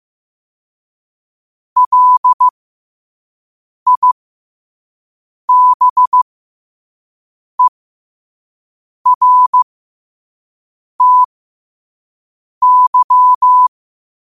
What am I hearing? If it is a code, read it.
Morse code, heard as a keyed tone, about 15 wpm, spaced as for 5 wpm, 1000 Hz. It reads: LIBERTY
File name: qotdfiles / 5wpm